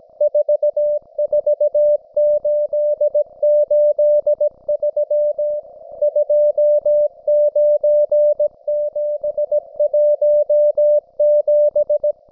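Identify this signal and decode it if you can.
Morse code, heard as a keyed tone, about 17 wpm, 590 Hz. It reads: 44883 29717
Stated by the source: Russian military Morse station, M32, RDL, on 5268 kHz.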